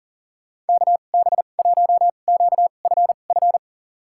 Morse code, heard as a keyed tone, about 40 wpm, spaced as for 29 wpm, 700 Hz. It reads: KB1QFF
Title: USA Call Signs - 6 Characters - 2x Character Spacing - 40wpm